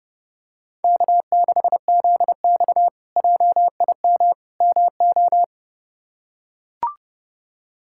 Morse code, heard as a keyed tone, about 30 wpm, 700 Hz. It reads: K6ZX JIM MO
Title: K1USN SST QSO Practice - 30wpm